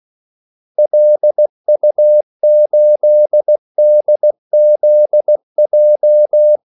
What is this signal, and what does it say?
Morse code, heard as a keyed tone, about 16 wpm, 600 Hz. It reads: LU8DZJ
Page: Kaldesignaler 16 wpm